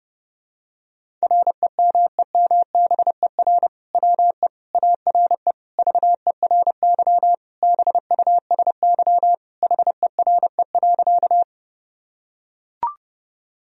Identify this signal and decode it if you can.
Morse code, heard as a keyed tone, about 30 wpm, 700 Hz. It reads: REMEMBER WE ARE VERY BUSY HERE.
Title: Sentences from Top 1000 Words - Review Part 3 - 30wpm